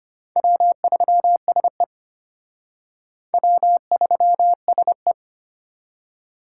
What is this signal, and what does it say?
Morse code, heard as a keyed tone, about 31 wpm, 700 Hz. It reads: W3SE W3SE
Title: Speed Racing - USA Call Signs 25wpm